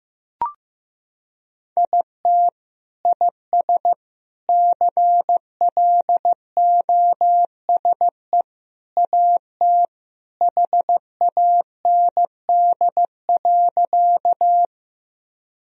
Morse code, heard as a keyed tone, about 15 wpm, 700 Hz. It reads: IT IS CLOSE AT HAND.
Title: Sentences from Top 1000 Words - Review Part 1 - 15wpm